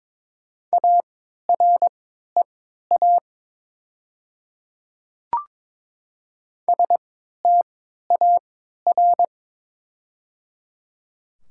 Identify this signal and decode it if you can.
Morse code, heard as a keyed tone, about 22 wpm, spaced as for 12 wpm, 700 Hz. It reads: AREA STAR
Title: 3x Character Spacing - Top 500 Words - 22wpm